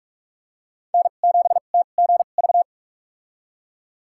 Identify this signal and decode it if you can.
Morse code, heard as a keyed tone, about 46 wpm, spaced as for 33 wpm, 700 Hz. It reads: N7TGV
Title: USA Call Signs - 2x Character Spacing - 45wpm